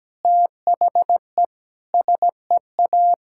Morse code, heard as a keyed tone, about 17 wpm, 700 Hz. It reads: THE SEA
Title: Sentences from Top 1000 Words - Review Part 5 - 17wpm